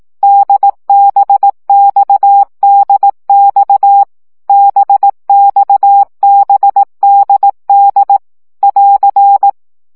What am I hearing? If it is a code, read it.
Morse code, heard as a keyed tone, about 18 wpm, 800 Hz. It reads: DBXDX BXBDD <AR>
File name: CW gyakorlas d b x 15 baud